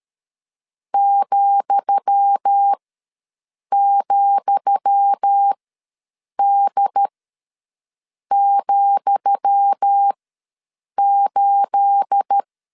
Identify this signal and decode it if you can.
Morse code, heard as a keyed tone, about 13 wpm, spaced as for 6 wpm, 800 Hz. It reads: ,,D,8
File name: LEZIONI / 10 SIGNS / 10D COMA ws 12wpm 3618